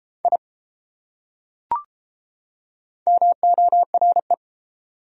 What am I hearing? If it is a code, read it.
Morse code, heard as a keyed tone, about 29 wpm, 700 Hz. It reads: I MORE